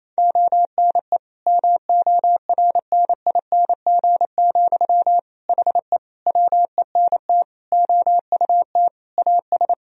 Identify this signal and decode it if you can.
Morse code, heard as a keyed tone, about 27 wpm, 700 Hz. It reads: ONE MORNING, HE WENT OUT AS
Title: Sentences from Top 1000 Words - Review Part 2 - 28wpm